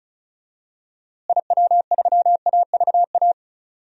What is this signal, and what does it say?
Morse code, heard as a keyed tone, about 36 wpm, 700 Hz. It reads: IW3AVA
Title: DXCC-100 Call Signs - 6 Characters - Repeated 2x - 35wpm